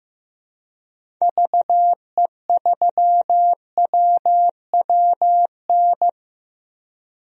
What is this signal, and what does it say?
Morse code, heard as a keyed tone, about 15 wpm, 700 Hz. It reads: VE3WWN
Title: DXCC-100 Call Signs - 6 Characters - 15wpm